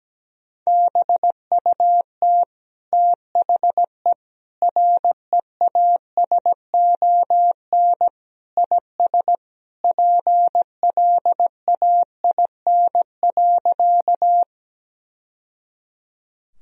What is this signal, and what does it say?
Morse code, heard as a keyed tone, about 17 wpm, 700 Hz. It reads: BUT THE REASON IS PLAIN.
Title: Sentences from Top 1000 Words - Review Part 2 - 17wpm